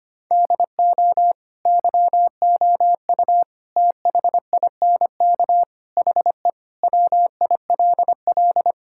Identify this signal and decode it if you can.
Morse code, heard as a keyed tone, about 25 wpm, 700 Hz. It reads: DO YOU THINK HE WILL
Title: Sentences from Top 1600 Words 25wpm